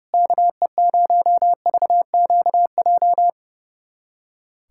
Morse code, heard as a keyed tone, about 30 wpm, 700 Hz. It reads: KE0VQJ
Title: USA Callsign DE Callsign 30wpm